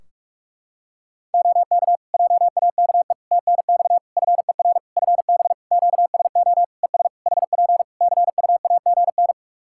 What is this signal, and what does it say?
Morse code, heard as a keyed tone, about 46 wpm, 700 Hz. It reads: OK JAKE TNX FER FB QSO ES HP CUAGN
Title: Practice QSO - Exchange 4 - Ending - 45wpm